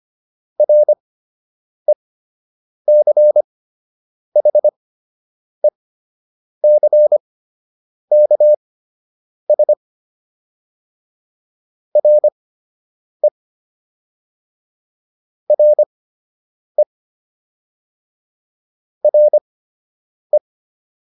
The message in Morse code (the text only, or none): RECHECKS RE RE RE